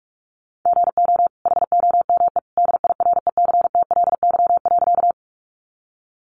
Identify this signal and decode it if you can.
Morse code, heard as a keyed tone, about 44 wpm, 700 Hz. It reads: GO HOME DIRECTLY.
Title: Sentences from Top 800 Words 45wpm